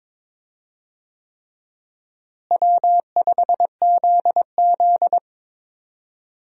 Morse code, heard as a keyed tone, about 22 wpm, 700 Hz. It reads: W5ZZ
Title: USA Call Signs - 4 Characters - Repeated 3x - 22wpm